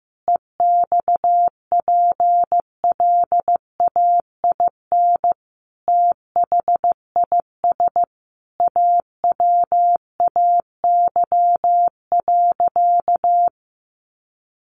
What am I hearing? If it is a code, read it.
Morse code, heard as a keyed tone, about 15 wpm, 700 Hz. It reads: EXPLAIN THIS AWAY.
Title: Sentences from Top 1100 Words 15wpm